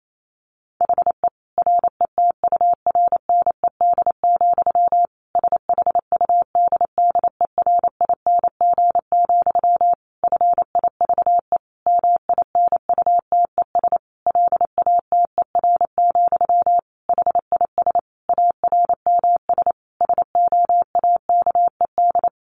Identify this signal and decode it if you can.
Morse code, heard as a keyed tone, about 28 wpm, 700 Hz. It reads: HE RETURNED, SHUDDERING, FIVE MINUTES LATER, HIS ARMS SOAKED
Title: A Sound Of Thunder 28wpm